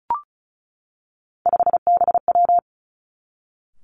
Morse code, heard as a keyed tone, about 36 wpm, 700 Hz. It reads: 5BW
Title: Three Challenging Letter-Number Combinations 35wpm